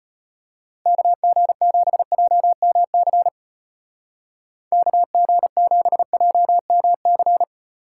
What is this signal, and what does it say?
Morse code, heard as a keyed tone, about 29 wpm, 700 Hz. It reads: KG7JMC KG7JMC
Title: Speed Racing - USA 2x3 Call Signs 25wpm